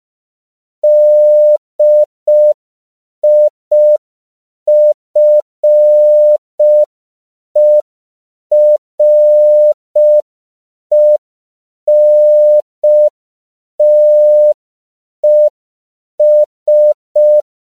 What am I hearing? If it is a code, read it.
Morse code, heard as a keyed tone, about 5 wpm, 600 Hz. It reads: DIFERENTES
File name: CW / 5PPM